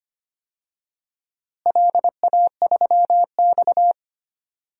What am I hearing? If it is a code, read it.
Morse code, heard as a keyed tone, about 25 wpm, 700 Hz. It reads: LA3X
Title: DXCC-100 Call Signs - 4 Characters - Repeated 2x - 25wpm